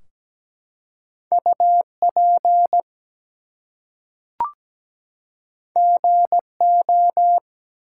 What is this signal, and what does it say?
Morse code, heard as a keyed tone, about 17 wpm, 700 Hz. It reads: UP GO
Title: Two Letter Words - 17wpm